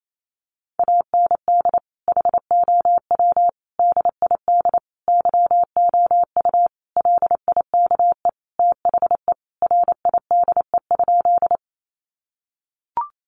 Morse code, heard as a keyed tone, about 28 wpm, 700 Hz. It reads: AND HOW DID YOU LIKE THE RIDE?
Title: Sentences from Top 1200 Words 28wpm